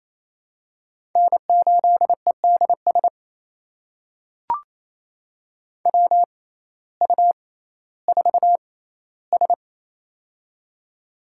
Morse code, heard as a keyed tone, about 28 wpm, spaced as for 15 wpm, 700 Hz. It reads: N8EDS WU4S
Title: USA Call Signs - 6x Character Spacing - 28wpm